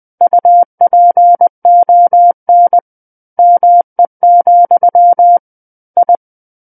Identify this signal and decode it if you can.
Morse code, heard as a keyed tone, about 20 wpm, 700 Hz. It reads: UPON ME, I
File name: 20 WPM